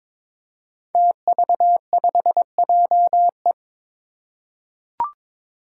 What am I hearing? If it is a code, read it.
Morse code, heard as a keyed tone, about 21 wpm, 700 Hz. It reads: TV5JE
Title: Five Challenging Letter-Number Combinations 22wpm